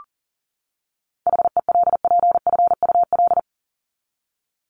Morse code, heard as a keyed tone, about 38 wpm, 700 Hz. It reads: HELPFUL
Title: Top 751-1000 Words 40wpm